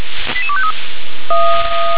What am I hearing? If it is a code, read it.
Five-tone selective call (ZVEI), heard as quick runs of five tones, tones of about 70 ms. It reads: 88253
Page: SDR/ZVEI